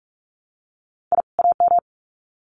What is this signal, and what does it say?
Morse code, heard as a keyed tone, about 44 wpm, 700 Hz. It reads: I AM
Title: Sets of 2 Words 45wpm